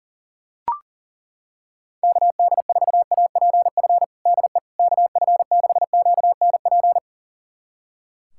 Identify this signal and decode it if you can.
Morse code, heard as a keyed tone, about 40 wpm, 700 Hz. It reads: KD4APF DE KF6QNP
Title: USA Callsign DE Callsign - 6 Characters - 40wpm